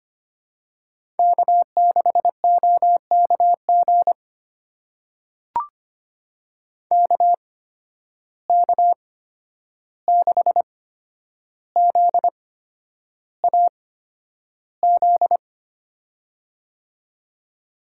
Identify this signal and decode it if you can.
Morse code, heard as a keyed tone, about 25 wpm, spaced as for 10 wpm, 700 Hz. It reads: K6OKG KK6ZAZ
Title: USA Call Signs - 8x Character Spacing - 25wpm